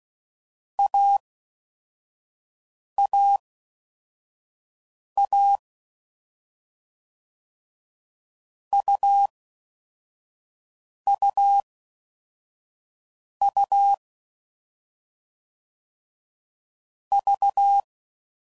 Morse code, heard as a keyed tone, about 16 wpm, 800 Hz. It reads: A A A U U U V
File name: Combinations / Difficult Combos / AUV ShortRun